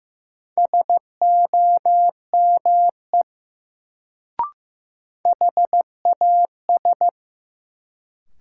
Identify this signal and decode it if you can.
Morse code, heard as a keyed tone, about 15 wpm, 700 Hz. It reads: SOME HAS
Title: Sentences from Top 100 Words - New Word Drill - 15wpm